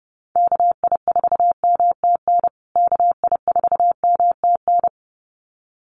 Morse code, heard as a keyed tone, about 29 wpm, 700 Hz. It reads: KI4MTN KI4MTN
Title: USA Call Signs Repeated 2x 30wpm